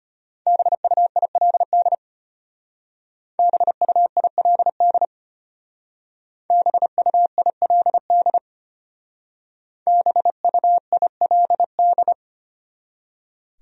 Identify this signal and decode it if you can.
Morse code, heard as a keyed tone, about 32 wpm, 700 Hz. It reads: BUILD BUILD BUILD BUILD